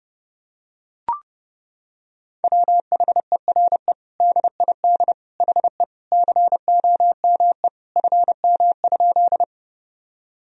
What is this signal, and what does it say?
Morse code, heard as a keyed tone, about 30 wpm, 700 Hz. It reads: WHERE DID HE COME FM?
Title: Sentences from Top 200 Words and Abbreviations 30wpm